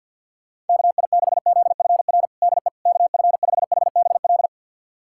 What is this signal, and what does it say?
Morse code, heard as a keyed tone, about 50 wpm, 700 Hz. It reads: KI6ZFR DE KF5HBL
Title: USA Callsign DE Callsign - 6 Characters - 50wpm